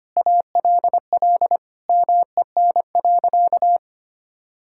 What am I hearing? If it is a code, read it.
Morse code, heard as a keyed tone, about 25 wpm, 700 Hz. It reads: ALL MEN.
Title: Sentences from Top 1000 Words - Review Part 6 - 25wpm